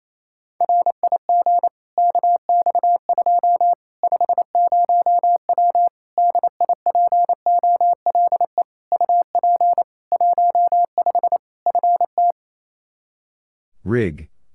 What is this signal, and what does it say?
Morse code, heard as a keyed tone, about 28 wpm, 700 Hz. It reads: RIG KX2 50W DIPOLE UP 15 FT
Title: K1USN SST QSO Practice 28wpm